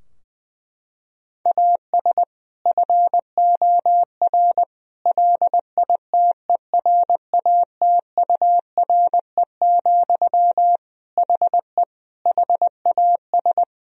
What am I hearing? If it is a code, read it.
Morse code, heard as a keyed tone, about 20 wpm, 700 Hz. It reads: AS FOR LITERATURE, HE HAS
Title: Sentences from Top 1000 Words - Review Part 3 - 20wpm